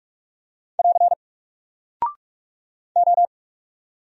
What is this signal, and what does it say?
Morse code, heard as a keyed tone, about 44 wpm, 700 Hz. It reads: <AR> O
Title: Single Character (Letter:Number:Punctuation:Prosign) Alt 45wpm